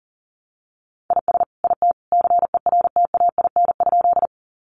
Morse code, heard as a keyed tone, about 40 wpm, 700 Hz. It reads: IS IT CERTAIN?
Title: Sentences from Top 500 Words 40wpm